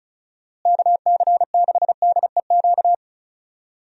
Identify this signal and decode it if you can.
Morse code, heard as a keyed tone, about 36 wpm, 700 Hz. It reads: KC6DEQ
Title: USA Call Signs - 6 Characters - Repeated 2x - 35wpm